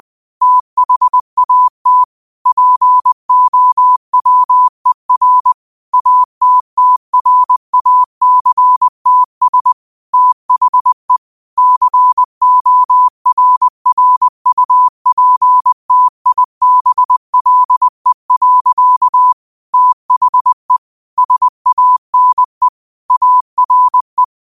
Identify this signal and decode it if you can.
Morse code, heard as a keyed tone, about 20 wpm, 1000 Hz. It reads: THAT POWER ATTRACTS THE CORRUPTIBLE. THE SANE ARE